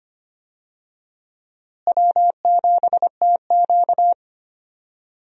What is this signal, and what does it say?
Morse code, heard as a keyed tone, about 25 wpm, 700 Hz. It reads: W7TQ
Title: USA Call Signs - 4 Characters - Repeated 3x - 25wpm